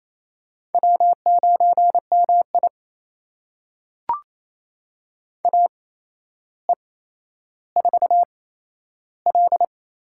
Morse code, heard as a keyed tone, about 28 wpm, spaced as for 12 wpm, 700 Hz. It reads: W9MI AE4L
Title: USA Call Signs - 4 Characters - 8x Character Spacing - 28wpm